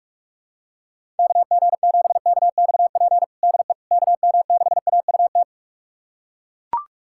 Morse code, heard as a keyed tone, about 42 wpm, 700 Hz. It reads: KG7KXP DE KM6AUT
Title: USA Callsign DE Callsign - 6 Characters - 45wpm